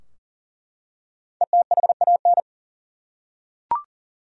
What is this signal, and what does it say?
Morse code, heard as a keyed tone, about 40 wpm, 700 Hz. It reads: ETHAN
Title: USA Common Names - Max 5 Letters - 40wpm